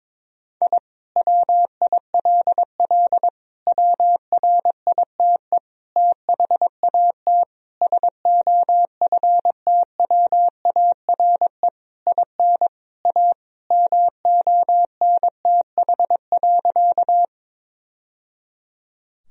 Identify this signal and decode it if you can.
Morse code, heard as a keyed tone, about 22 wpm, 700 Hz. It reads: I WILL WRITE THAT SOFTWARE IN A MONTH.
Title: Sentences from Top 1000 Words - Review Part 3 - 22wpm